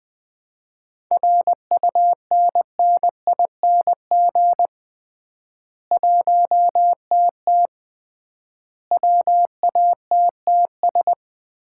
Morse code, heard as a keyed tone, about 21 wpm, 700 Hz. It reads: RUNNING 1TT WATTS
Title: Common QSO Phrases - 3x Word-Spacing - 20wpm